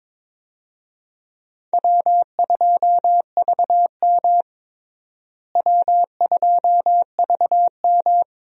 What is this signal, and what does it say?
Morse code, heard as a keyed tone, about 22 wpm, 700 Hz. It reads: W2VM W2VM
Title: USA Call Signs - 4 Characters - Repeated 3x - 22wpm